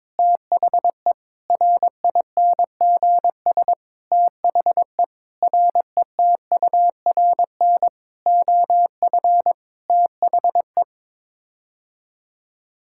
Morse code, heard as a keyed tone, about 22 wpm, 700 Hz. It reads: THE RINGS THE RETURN OF THE
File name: Sets of 9 Words - 22wpm